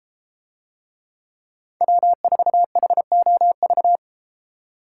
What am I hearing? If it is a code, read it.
Morse code, heard as a keyed tone, about 32 wpm, 700 Hz. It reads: W4HOV